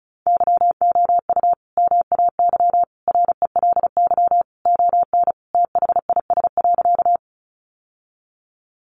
Morse code, heard as a keyed tone, about 34 wpm, 700 Hz. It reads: YOU MAY RELY ON THIS.
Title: Sentences from Top 1200 Words 35wpm